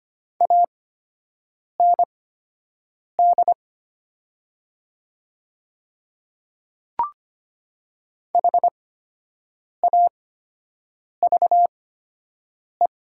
Morse code, heard as a keyed tone, about 25 wpm, spaced as for 7 wpm, 700 Hz. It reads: AND HAVE
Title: 8x Character Spacing - Top 200 Words - 25wpm